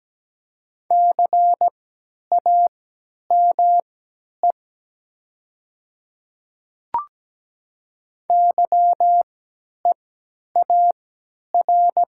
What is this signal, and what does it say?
Morse code, heard as a keyed tone, about 17 wpm, spaced as for 10 wpm, 700 Hz. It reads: CAME YEAR
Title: 3x Character Spacing - Top 500 Words - 17wpm